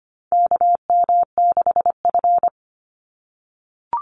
Morse code, heard as a keyed tone, about 25 wpm, 700 Hz. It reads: KM6F
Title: USA Call Signs 25wpm